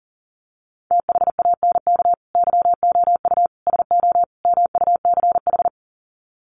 Morse code, heard as a keyed tone, about 40 wpm, 700 Hz. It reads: THANK YOU SO MUCH
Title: Sets of 4 Words 40wpm